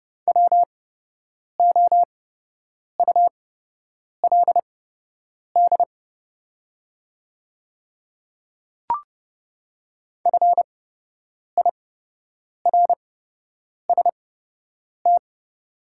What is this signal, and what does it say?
Morse code, heard as a keyed tone, about 30 wpm, spaced as for 8 wpm, 700 Hz. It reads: WOULD FIRST
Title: 8x Character Spacing - Top 100 Words - 30wpm